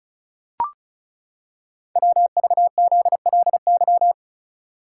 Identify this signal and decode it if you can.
Morse code, heard as a keyed tone, about 36 wpm, 700 Hz. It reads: WVZLY
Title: Code Groups - 5 Letters - 35wpm